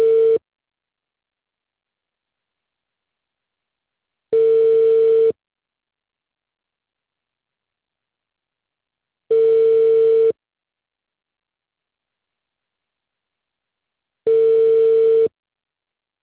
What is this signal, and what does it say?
Ringback tone (China, 450 Hz).